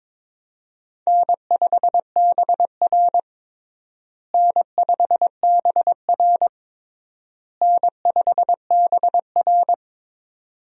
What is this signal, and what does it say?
Morse code, heard as a keyed tone, about 22 wpm, 700 Hz. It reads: N5BR N5BR N5BR
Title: USA Call Signs - 4 Characters - Repeated 3x - 22wpm